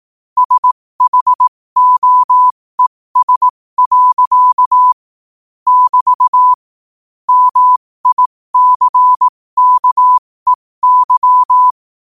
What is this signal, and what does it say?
Morse code, heard as a keyed tone, about 18 wpm, spaced as for 15 wpm, 1000 Hz. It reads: SHOES. <BT> MICKEY
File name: qotdfiles / 15wpm